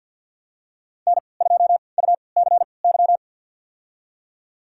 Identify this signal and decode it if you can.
Morse code, heard as a keyed tone, about 51 wpm, spaced as for 28 wpm, 700 Hz. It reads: N2UCY
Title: USA Call Signs - 5 Characters - 3x Character Spacing - 50wpm